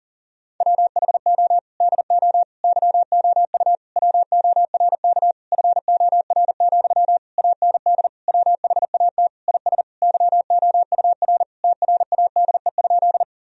Morse code, heard as a keyed tone, about 40 wpm, 700 Hz. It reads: WHO DO YOU WORK FOR, AND WHAT IS YOUR TRADE?
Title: Sentences from Top 1000 Words - Review Part 4 - 40wpm